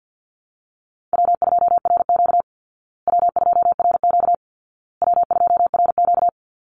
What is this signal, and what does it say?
Morse code, heard as a keyed tone, about 50 wpm, 700 Hz. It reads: W2RQ W2RQ W2RQ
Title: Call Signs Repeated 2x - 50wpm